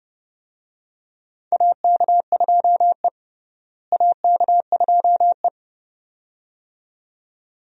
Morse code, heard as a keyed tone, about 30 wpm, 700 Hz. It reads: AK2E AK2E
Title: USA Call Signs - 4 Characters - Repeated 2x - 30wpm